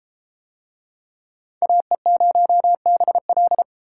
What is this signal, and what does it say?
Morse code, heard as a keyed tone, about 32 wpm, 700 Hz. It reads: AE0BL